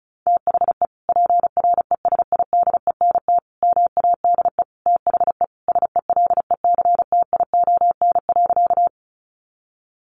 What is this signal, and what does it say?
Morse code, heard as a keyed tone, about 34 wpm, 700 Hz. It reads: THE PRESIDENT MADE THE SELECTION.
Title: Sentences from Top 1400 Words 35wpm